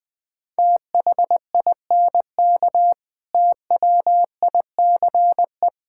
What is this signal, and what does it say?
Morse code, heard as a keyed tone, about 21 wpm, 700 Hz. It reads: THINK TWICE
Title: Difficult Sentences with Hints - 20wpm